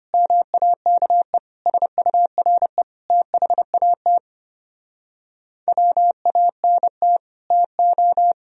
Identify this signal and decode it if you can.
Morse code, heard as a keyed tone, about 33 wpm, 700 Hz. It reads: MAKE SURE THAT WANT TO